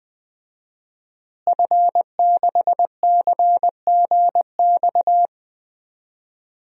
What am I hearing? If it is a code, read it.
Morse code, heard as a keyed tone, about 20 wpm, 700 Hz. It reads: F6CGX
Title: DXCC-100 Call Signs - 5 Characters - Repeated 3x - 20wpm